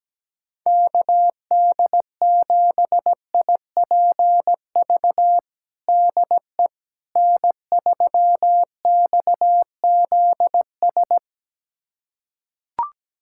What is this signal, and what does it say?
Morse code, heard as a keyed tone, about 17 wpm, 700 Hz. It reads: KD7IPV DE N3XZS
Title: USA Callsign DE Callsign 17wpm